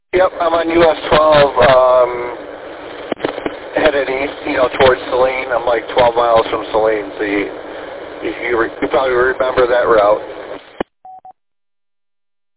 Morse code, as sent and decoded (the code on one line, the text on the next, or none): none